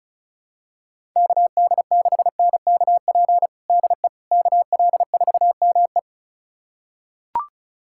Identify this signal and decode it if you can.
Morse code, heard as a keyed tone, about 34 wpm, 700 Hz. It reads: KD6NKP DE KL4ME
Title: USA Callsign DE Callsign 35wpm